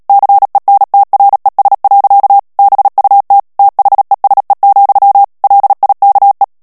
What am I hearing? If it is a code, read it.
Morse code, heard as a keyed tone, about 37 wpm, 800 Hz. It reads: CENTRES. BUT THESE, LIKE
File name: morse mp3 / ANGLAIS texte clair 2 vitesse 35 mots minute